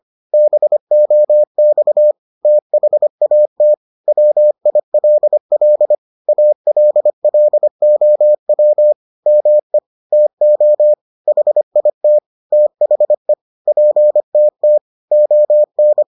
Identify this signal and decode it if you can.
Morse code, heard as a keyed tone, about 25 wpm, 600 Hz. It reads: BOX THAT WILL ALLOW ME TO HIT THE PTT ON